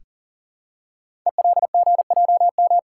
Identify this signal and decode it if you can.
Morse code, heard as a keyed tone, about 41 wpm, 690 Hz. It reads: ELGJM
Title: Code Groups - 5 Letters - 40wpm